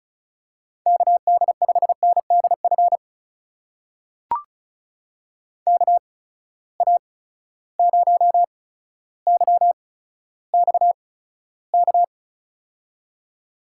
Morse code, heard as a keyed tone, about 36 wpm, spaced as for 15 wpm, 700 Hz. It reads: KD5NDF KA0YXK
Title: USA Call Signs - 6 Characters - 8x Character Spacing - 35wpm